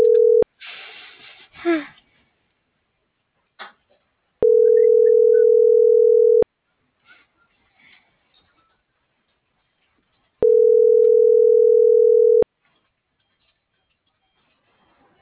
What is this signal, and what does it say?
Ringback tone (North America), 440 + 480 Hz.